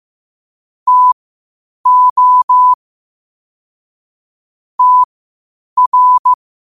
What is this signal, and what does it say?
Morse code, heard as a keyed tone, about 15 wpm, spaced as for 8 wpm, 1000 Hz. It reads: TO TR
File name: qotdfiles / 8wpm